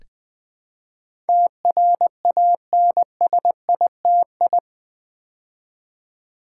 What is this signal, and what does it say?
Morse code, heard as a keyed tone, about 20 wpm, 700 Hz. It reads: TRANSITI
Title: Build A Word - 4 Letters At A Time - 10-Letter Words - 20wpm